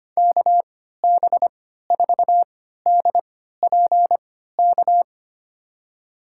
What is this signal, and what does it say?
Morse code, heard as a keyed tone, about 25 wpm, spaced as for 14 wpm, 700 Hz. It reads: KB4DPK
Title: USA Call Signs - 6 Characters - 3x Character Spacing - 25wpm